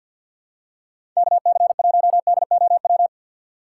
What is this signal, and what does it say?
Morse code, heard as a keyed tone, about 53 wpm, 700 Hz. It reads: KC1DOW
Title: USA Call Signs - 6 Characters - 4x Character Spacing - 50wpm